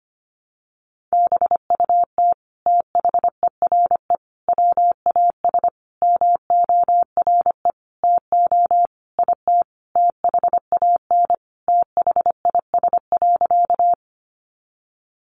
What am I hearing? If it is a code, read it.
Morse code, heard as a keyed tone, about 25 wpm, 700 Hz. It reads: BUT THERE WAS MORE TO IT THAN THIS.
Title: Sentences from Top 100 Words 25wpm